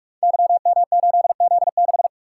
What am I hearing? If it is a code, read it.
Morse code, heard as a keyed tone, about 44 wpm, 700 Hz. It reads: YM8Z6
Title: Code Groups - 5 Letter:Numbers - 45wpm